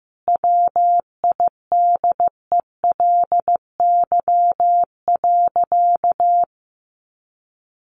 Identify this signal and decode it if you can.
Morse code, heard as a keyed tone, about 14 wpm, 700 Hz. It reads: WIDELY.
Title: Sentences from Top 1200 Words 15wpm